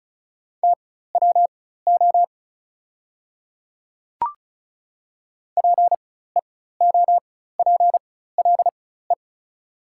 Morse code, heard as a keyed tone, about 34 wpm, spaced as for 16 wpm, 700 Hz. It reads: TWO PEOPLE
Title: 4x Character Spacing - Top 100 Words - 35wpm